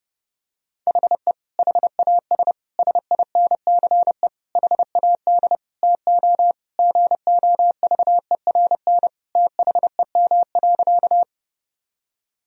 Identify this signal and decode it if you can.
Morse code, heard as a keyed tone, about 30 wpm, 700 Hz. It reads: HE HAS SINCE HAD TO GOVERN THEM.
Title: Sentences from Top 1000 Words - Review Part 2 - 30wpm